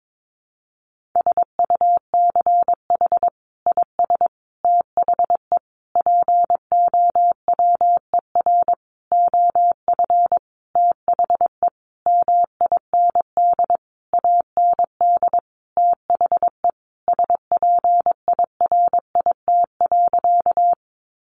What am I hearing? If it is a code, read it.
Morse code, heard as a keyed tone, about 22 wpm, 700 Hz. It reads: SUCH IS THE POWER OF THE MIND AND THE SPIRIT.
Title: Sentences from Top 1400 Words 22wpm